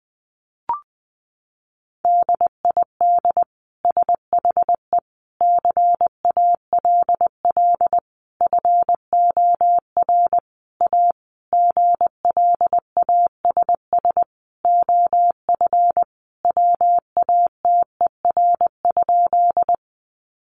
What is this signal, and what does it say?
Morse code, heard as a keyed tone, about 20 wpm, 700 Hz. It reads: DID SHE CALL FOR A GLASS OF WATER?
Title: Sentences from Top 1300 Words 20wpm